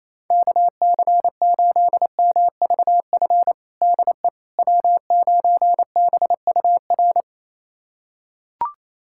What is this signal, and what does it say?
Morse code, heard as a keyed tone, about 28 wpm, 700 Hz. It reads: KC8MVF DE W9BUR
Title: USA Callsign DE Callsign 28wpm